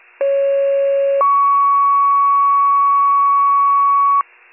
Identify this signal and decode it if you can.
Two-tone page, heard as a one-second tone, then a three-second tone, about 550 then 1120 Hz. It is heard from 0.2 to 4.2 s.